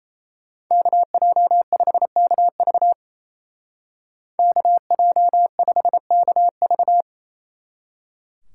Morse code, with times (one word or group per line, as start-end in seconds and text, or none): 0.7-2.9 KJ5KV
4.4-7.0 KJ5KV